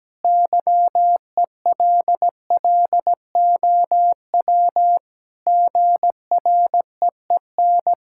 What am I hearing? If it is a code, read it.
Morse code, heard as a keyed tone, about 17 wpm, 700 Hz. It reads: YELLOW GREEN
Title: Pangram Challenge v2 - 17wpm